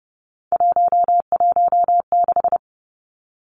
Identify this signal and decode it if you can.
Morse code, heard as a keyed tone, about 30 wpm, 700 Hz. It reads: 116
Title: Numbers - 3-Digit 30wpm